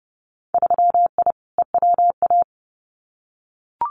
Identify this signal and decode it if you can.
Morse code, heard as a keyed tone, about 30 wpm, 700 Hz. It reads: 3I EWA
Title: Winter Field Day Exchanges Encore 30wpm